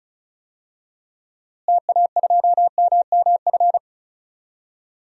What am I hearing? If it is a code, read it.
Morse code, heard as a keyed tone, about 36 wpm, 700 Hz. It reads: TA2MMF
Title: DXCC-100 Call Signs - 6 Characters - Repeated 3x - 35wpm